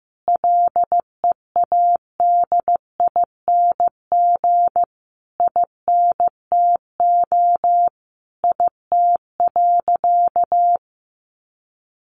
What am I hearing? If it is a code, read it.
Morse code, heard as a keyed tone, about 15 wpm, 700 Hz. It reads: LEADING INTO IT.
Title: Sentences from Top 1100 Words 15wpm